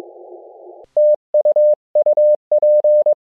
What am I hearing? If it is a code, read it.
Morse code, heard as a keyed tone, about 22 wpm, 600 Hz. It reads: TU UP
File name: jh6nkz-g8bcg